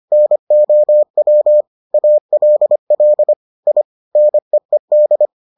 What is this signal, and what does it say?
Morse code, heard as a keyed tone, about 25 wpm, 600 Hz. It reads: NOW ALL I NEED